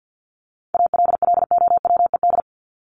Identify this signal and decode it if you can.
Morse code, heard as a keyed tone, about 50 wpm, 700 Hz. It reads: ALLOWED
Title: ED Suffix - Top 100 Words with Base Words 50wpm